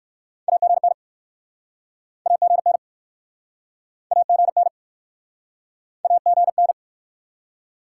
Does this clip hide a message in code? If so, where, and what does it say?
Yes: Morse code from 0.5 to 6.7 s, heard as a keyed tone, about 61 wpm, 700 Hz. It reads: AGN AGN AGN AGN